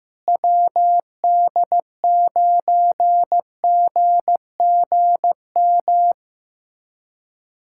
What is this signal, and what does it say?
Morse code, heard as a keyed tone, about 15 wpm, 700 Hz. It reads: WD9GGM
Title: USA Callsign DE Callsign - 6 Characters - 15wpm